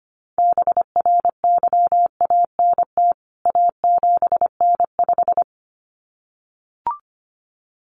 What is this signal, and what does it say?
Morse code, heard as a keyed tone, about 25 wpm, 700 Hz. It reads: BRYANT A7N5
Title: CWT Exchange 25wpm